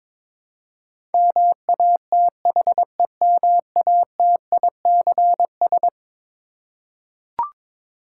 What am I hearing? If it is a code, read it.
Morse code, heard as a keyed tone, about 22 wpm, 700 Hz. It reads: MATHEMATICS
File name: practice / 22wpm / Eleven Letter Words - 22wpm